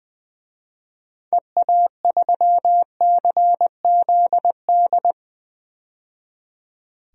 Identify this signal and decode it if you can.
Morse code, heard as a keyed tone, about 20 wpm, 700 Hz. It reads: EA3CZD
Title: DXCC-100 Call Signs - 6 Characters - 20wpm